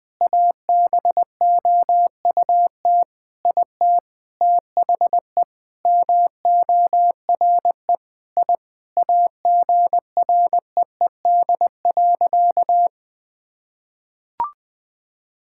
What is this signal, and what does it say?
Morse code, heard as a keyed tone, about 20 wpm, 700 Hz. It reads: ABOUT IT THE MORE I AGREED.
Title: Sentences with 9-Word Phrases - 20wpm